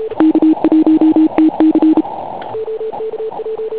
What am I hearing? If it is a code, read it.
Morse code, heard as a keyed tone, about 33 wpm, 320 Hz. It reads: K1TC
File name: Pile UP CUC204